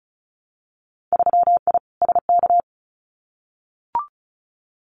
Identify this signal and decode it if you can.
Morse code, heard as a keyed tone, about 36 wpm, 700 Hz. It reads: 3I SK
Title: Winter Field Day Exchanges Encore 35wpm